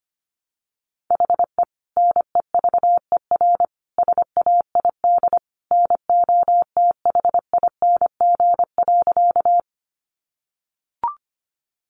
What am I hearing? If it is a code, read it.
Morse code, heard as a keyed tone, about 25 wpm, 700 Hz. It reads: HE NEVER SAID NOTHING.
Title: Sentences from Top 500 Words 25wpm